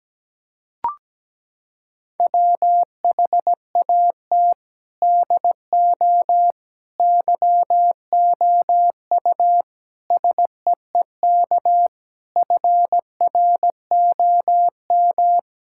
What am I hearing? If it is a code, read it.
Morse code, heard as a keyed tone, about 17 wpm, 700 Hz. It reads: WHAT DO YOU SEEK FROM
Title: Sentences from Top 1600 Words 17wpm